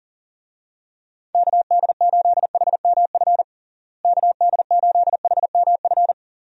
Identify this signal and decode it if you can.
Morse code, heard as a keyed tone, about 40 wpm, 700 Hz. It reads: KD8HMF KD8HMF
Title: USA Call Signs - 6 Characters - Repeated 2x - 40wpm